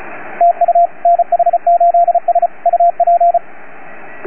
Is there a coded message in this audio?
Yes: Morse code from 0.4 to 3.4 s, heard as a keyed tone, about 33 wpm, 690 Hz, with radio noise. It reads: TU NH8S UP